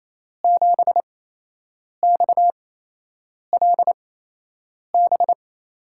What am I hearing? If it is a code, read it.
Morse code, heard as a keyed tone, about 28 wpm, spaced as for 8 wpm, 700 Hz. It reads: 7XLB
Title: USA Call Signs - 8x Character Spacing - 28wpm